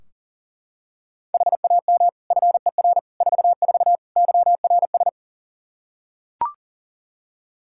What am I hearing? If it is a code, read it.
Morse code, heard as a keyed tone, about 40 wpm, 700 Hz. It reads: HAM FER 44 YRS
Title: Practice QSO - Exchange 3 - More Info - 40wpm